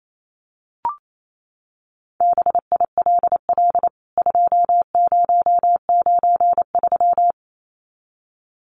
Morse code, heard as a keyed tone, about 28 wpm, 700 Hz. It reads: BILL 2093
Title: CWT Exchange 28wpm